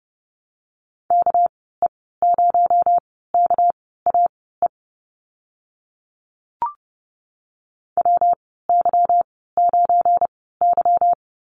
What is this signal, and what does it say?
Morse code, heard as a keyed tone, about 30 wpm, spaced as for 17 wpm, 700 Hz. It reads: KE0KAE WY9Y
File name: Lesson 37g - Letter x - Call Signs 30wpm